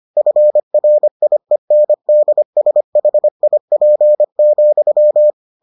Morse code, heard as a keyed tone, about 25 wpm, 600 Hz. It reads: FRIENDSHIP,